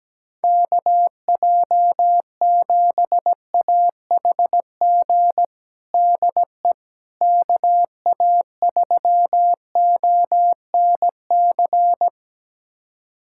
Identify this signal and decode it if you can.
Morse code, heard as a keyed tone, about 17 wpm, 700 Hz. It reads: KJ7AHG DE KA3ONC
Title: USA Callsign DE Callsign 17wpm